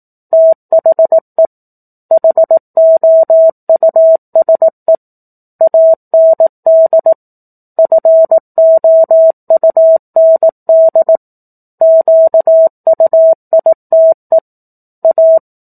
Morse code, heard as a keyed tone, about 18 wpm, 650 Hz. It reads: THE HOUSE AND FOUND QUITE A
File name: 04-Chapter-04 18 WPM